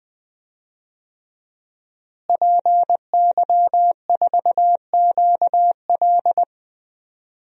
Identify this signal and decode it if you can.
Morse code, heard as a keyed tone, about 20 wpm, 700 Hz. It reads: PY4QL
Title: DXCC-100 Call Signs - 5 Characters - Repeated 2x - 20wpm